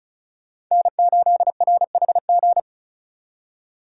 Morse code, heard as a keyed tone, about 36 wpm, 700 Hz. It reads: N8RHG
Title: USA Call Signs - 5 Characters - 4x Character Spacing - 35wpm